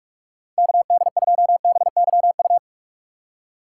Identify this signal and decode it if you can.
Morse code, heard as a keyed tone, about 44 wpm, 700 Hz. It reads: KD2BYU
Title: USA Call Signs - 6 Characters - 6x Character Spacing - 45wpm